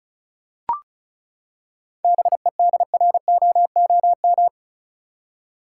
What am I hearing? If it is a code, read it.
Morse code, heard as a keyed tone, about 36 wpm, 700 Hz. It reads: BEDROOM
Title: Seven Letter Words - 35wpm